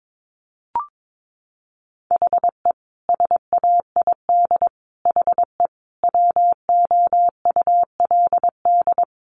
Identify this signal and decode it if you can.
Morse code, heard as a keyed tone, about 22 wpm, 700 Hz. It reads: HE SAID HE WOULD
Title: Sentences from Top 1200 Words 22wpm